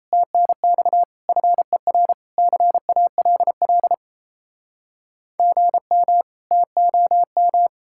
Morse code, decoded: TNX FER CALL GM TOM

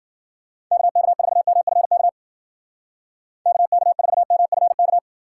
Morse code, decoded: KK4MFK KK4MFK

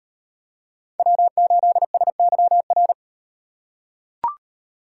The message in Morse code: W8SYR

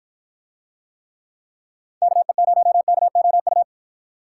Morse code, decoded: KE0KOU

